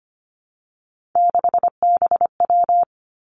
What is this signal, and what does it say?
Morse code, heard as a keyed tone, about 25 wpm, 700 Hz. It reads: 6BW
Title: Three Challenging Letter-Number Combinations 25wpm